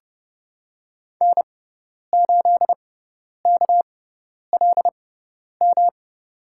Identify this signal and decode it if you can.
Morse code, heard as a keyed tone, about 32 wpm, spaced as for 10 wpm, 700 Hz. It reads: N8KLM
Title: USA Call Signs - 6x Character Spacing - 30wpm